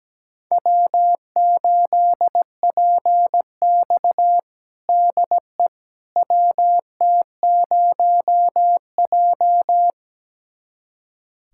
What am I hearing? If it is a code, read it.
Morse code, heard as a keyed tone, about 17 wpm, 700 Hz. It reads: W8PX DE WT0J
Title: USA Callsign DE Callsign - 4 Characters - 17wpm